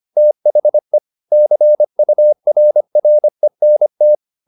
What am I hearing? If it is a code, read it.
Morse code, heard as a keyed tone, about 25 wpm, 600 Hz. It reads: THE CURRENT
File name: the old man and the sea part 1 25wpm